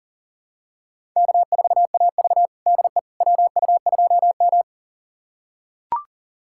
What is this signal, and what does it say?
Morse code, heard as a keyed tone, about 40 wpm, 700 Hz. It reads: K4AV DE WU2M
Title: USA Callsign DE Callsign - 4 Characters - 40wpm